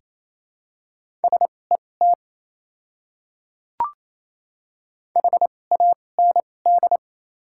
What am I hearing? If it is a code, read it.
Morse code, heard as a keyed tone, about 28 wpm, spaced as for 20 wpm, 700 Hz. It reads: SET HAND
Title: 2x Character Spacing - Top 200 Words - 28wpm